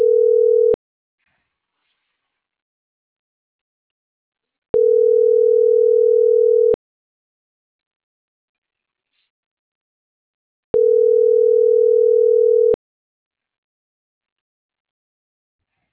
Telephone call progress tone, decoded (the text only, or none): ringback tone